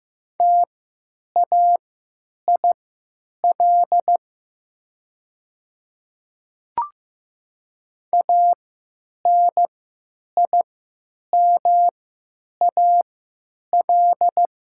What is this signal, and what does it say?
Morse code, heard as a keyed tone, about 15 wpm, spaced as for 9 wpm, 700 Hz. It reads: TAIL ANIMAL
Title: 3x Character Spacing - Top 500 Words - 15wpm